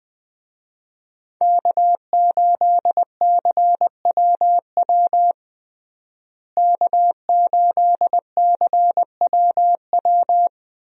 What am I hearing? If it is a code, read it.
Morse code, heard as a keyed tone, about 20 wpm, 700 Hz. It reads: K8CWW K8CWW
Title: USA Call Signs - 5 Characters - Repeated 2x - 20wpm